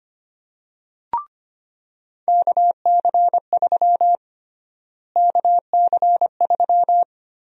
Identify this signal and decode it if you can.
Morse code, heard as a keyed tone, about 25 wpm, 700 Hz. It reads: KC3 KC3
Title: USA Call Sign Prefixes - Repeated 2x - 25wpm